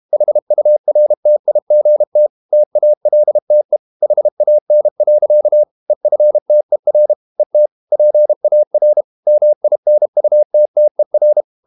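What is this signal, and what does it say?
Morse code, heard as a keyed tone, about 32 wpm, 600 Hz. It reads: HURTIGT TALTE HAN. EFTER ET PAR MINUTTER
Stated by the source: High-speed QRQ CW at 32 WPM.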